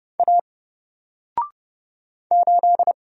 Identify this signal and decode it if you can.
Morse code, heard as a keyed tone, about 29 wpm, 700 Hz. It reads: A 8
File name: Single-Letter-Number-30wpm